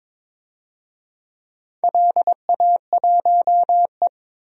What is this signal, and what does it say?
Morse code, heard as a keyed tone, about 22 wpm, 700 Hz. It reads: LA1E
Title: DXCC-100 Call Signs - 4 Characters - 22wpm